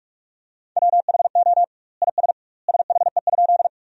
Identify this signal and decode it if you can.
Morse code, heard as a keyed tone, about 42 wpm, 700 Hz. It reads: WHO IS SHE?
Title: 2x Word-Spacing - Sentences from Top 100 Words - 45wpm